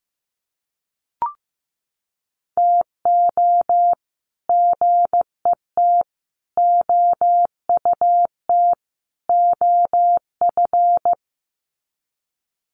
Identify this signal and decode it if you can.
Morse code, heard as a keyed tone, about 15 wpm, 700 Hz. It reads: TO GET OUT OF
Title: Sets of 4 Words 15wpm